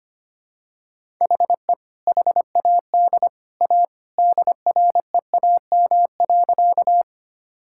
Morse code, heard as a keyed tone, about 25 wpm, 700 Hz. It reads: HE HAD A DREAM.
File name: practice / 25wpm / Sentences with 2-Word Phrases - 25wpm